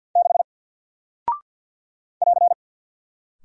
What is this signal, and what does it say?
Morse code, heard as a keyed tone, about 48 wpm, 700 Hz. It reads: 6 <AR>
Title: Single Character (Letter:Number:Punctuation:Prosign) Alt 50wpm